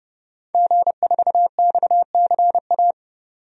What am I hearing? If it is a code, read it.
Morse code, heard as a keyed tone, about 30 wpm, 700 Hz. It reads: G4XCA
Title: DXCC-100 Call Signs - 5 Characters - 30wpm